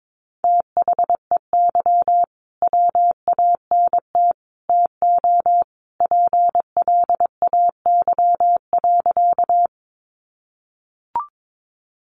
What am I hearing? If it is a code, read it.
Morse code, heard as a keyed tone, about 22 wpm, 700 Hz. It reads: THEY WANT TO PLAY.
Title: Easy Phrases 22wpm